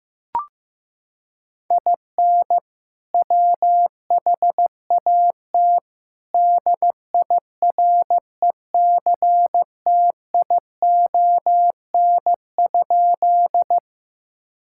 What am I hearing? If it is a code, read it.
Morse code, heard as a keyed tone, about 15 wpm, 700 Hz. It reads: IN WHAT DIRECTION?
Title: Sentences from Top 1000 Words - Review Part 5 - 15wpm